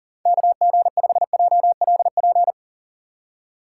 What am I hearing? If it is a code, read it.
Morse code, heard as a keyed tone, about 38 wpm, 700 Hz. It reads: KG5JLP
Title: USA Call Signs - 6 Characters - 4x Character Spacing - 40wpm